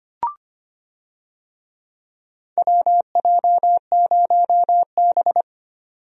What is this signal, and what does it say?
Morse code, heard as a keyed tone, about 25 wpm, 700 Hz. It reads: WJ0B
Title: USA Call Signs - 4 Characters - 25wpm